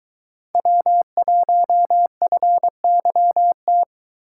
Morse code, heard as a keyed tone, about 22 wpm, 700 Hz. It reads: W1FYT